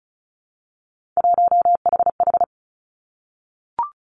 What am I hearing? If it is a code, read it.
Morse code, heard as a keyed tone, about 34 wpm, 700 Hz. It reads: 1HH
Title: Three Challenging Letter-Number Combinations 35wpm